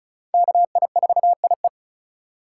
Morse code, heard as a keyed tone, about 36 wpm, 700 Hz. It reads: KI4IE
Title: USA Call Signs - 5 Characters - 2x Character Spacing - 35wpm